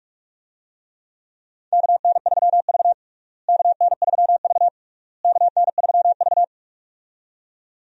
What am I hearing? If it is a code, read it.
Morse code, heard as a keyed tone, about 46 wpm, 700 Hz. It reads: KN3V KN3V KN3V
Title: USA Call Signs - 4 Characters - Repeated 3x - 45wpm